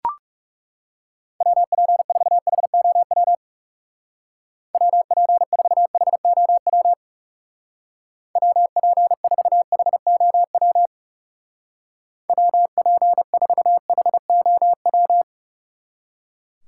Morse code, heard as a keyed tone, about 38 wpm, 700 Hz. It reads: WP4HOW WP4HOW WP4HOW WP4HOW
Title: Speed Racing - USA 2x3 Call Signs 30wpm